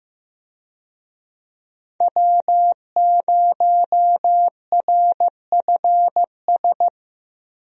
Morse code, heard as a keyed tone, about 16 wpm, 700 Hz. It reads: W0RFS
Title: USA Call Signs - 5 Characters - 15wpm